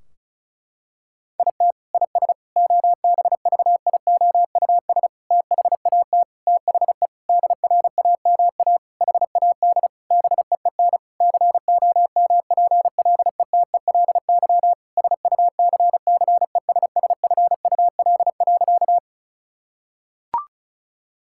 Morse code, heard as a keyed tone, about 36 wpm, 700 Hz. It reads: IT IS OBVIOUS THAT THE DRAMA HAD BEEN COMPLETELY SUCCESSFUL.
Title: Sentences from Top 1600 Words 35wpm